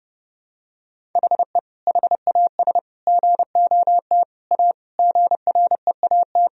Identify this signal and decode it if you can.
Morse code, heard as a keyed tone, about 30 wpm, 700 Hz. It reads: HE HAS GOT A GREAT
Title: Sentences from Top 1000 Words - Review Part 1 - 30wpm